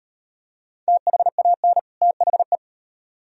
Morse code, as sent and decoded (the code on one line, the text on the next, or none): - .... .- -. / - .... .
THAN THE